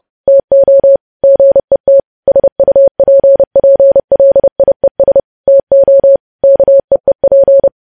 Morse code, heard as a keyed tone, about 30 wpm, 550 Hz. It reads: TO GET SUPPLIES TO KEEP
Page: news headlines in morse code at 30 wpm